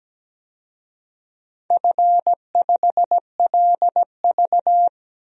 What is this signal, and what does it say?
Morse code, heard as a keyed tone, about 17 wpm, 700 Hz. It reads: F5LV
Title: DXCC-100 Call Signs - 4 Characters - 17wpm